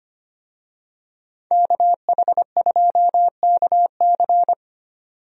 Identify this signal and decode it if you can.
Morse code, heard as a keyed tone, about 25 wpm, 700 Hz. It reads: KH2KC
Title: USA Call Signs - 5 Characters - 25wpm